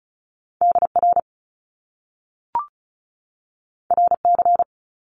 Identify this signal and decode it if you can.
Morse code, heard as a keyed tone, about 36 wpm, 700 Hz. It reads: DR RC
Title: Top Two Letter Combinations in Words 35wpm